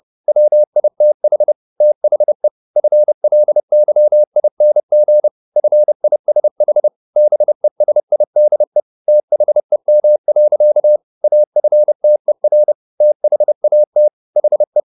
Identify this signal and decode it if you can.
Morse code, heard as a keyed tone, about 30 wpm, 600 Hz. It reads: WITH THE FLYING FISH BESIDE THEM. AFTER THAT HE